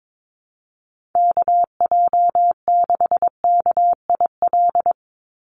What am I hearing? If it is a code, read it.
Morse code, heard as a keyed tone, about 22 wpm, 700 Hz. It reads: KJ6KIL
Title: USA Call Signs 22wpm